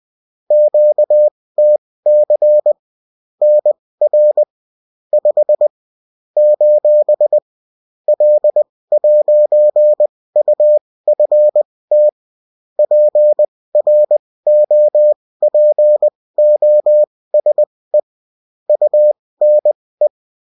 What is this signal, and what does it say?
Morse code, heard as a keyed tone, about 20 wpm, spaced as for 16 wpm, 600 Hz. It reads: QTC NR 5 : L'UFT PROPOSE UNE
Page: bulletin-uft-sous-forme-texte-et-en-cw